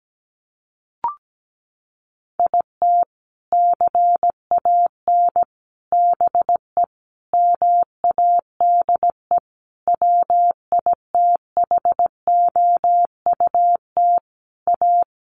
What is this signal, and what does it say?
Morse code, heard as a keyed tone, about 17 wpm, 700 Hz. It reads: IT CAN BE MADE WITHOUT A
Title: Sentences from Top 1200 Words 17wpm